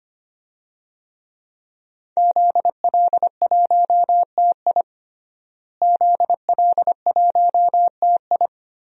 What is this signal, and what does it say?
Morse code, heard as a keyed tone, about 25 wpm, 700 Hz. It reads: ZL1TI ZL1TI
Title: DXCC-100 Call Signs - 5 Characters - Repeated 2x - 25wpm